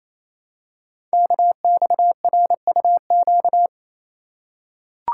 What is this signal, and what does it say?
Morse code, heard as a keyed tone, about 28 wpm, 700 Hz. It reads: KXRUQ
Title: Code Groups - 5 Letters - 28wpm